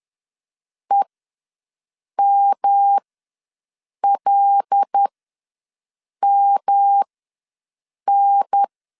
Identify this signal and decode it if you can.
Morse code, heard as a keyed tone, about 10 wpm, spaced as for 6 wpm, 800 Hz. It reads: EMLMN
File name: LEZIONI / 5 LBVC / 5 L ws 10wpm 3636